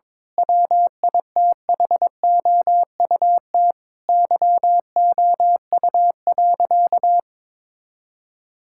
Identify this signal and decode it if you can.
Morse code, heard as a keyed tone, about 22 wpm, 700 Hz. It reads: WITHOUT YOU.
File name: Difficult Sentences with Hints - 22wpm